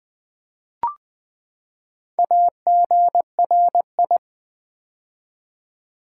Morse code, heard as a keyed tone, about 20 wpm, 700 Hz. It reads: AGRI
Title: Build A Word - 4 Letters At A Time - 12-Letter Words - 20wpm